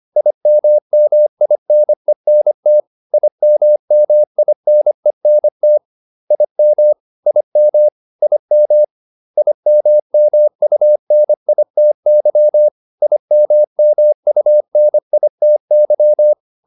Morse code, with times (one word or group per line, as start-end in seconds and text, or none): 0.2-2.8 IMMINENT
3.1-5.8 IMMINENT
6.3-6.9 IM
7.3-7.9 IM
8.2-8.8 IM
9.4-12.7 IMMUNITY
13.0-16.3 IMMUNITY